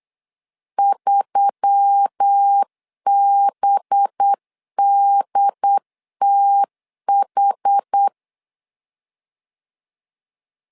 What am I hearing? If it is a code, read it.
Morse code, heard as a keyed tone, about 8 wpm, 800 Hz. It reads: 3BDTH
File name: LEZIONI / 5 LBVC / 5 LB 8wpm 2816